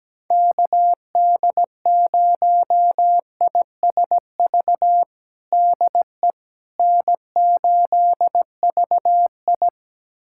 Morse code, heard as a keyed tone, about 17 wpm, 700 Hz. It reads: KD0ISV DE N8VI
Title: USA Callsign DE Callsign 17wpm